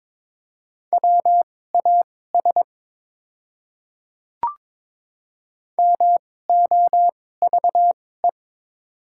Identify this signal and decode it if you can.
Morse code, heard as a keyed tone, about 22 wpm, spaced as for 16 wpm, 700 Hz. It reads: WAS MOVE
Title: 2x Character Spacing - Top 300 Words - 22wpm